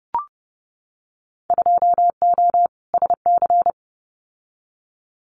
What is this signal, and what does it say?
Morse code, heard as a keyed tone, about 31 wpm, 700 Hz. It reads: 2O SC
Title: Winter Field Day Exchanges Encore 30wpm